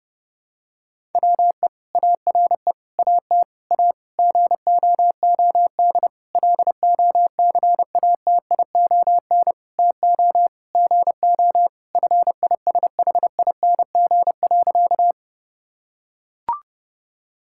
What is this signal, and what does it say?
Morse code, heard as a keyed tone, about 30 wpm, 700 Hz. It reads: WE ARE AT A GOOD LOCATION TO GO FISHING.
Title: Sentences from Top 1000 Words - Review Part 6 - 30wpm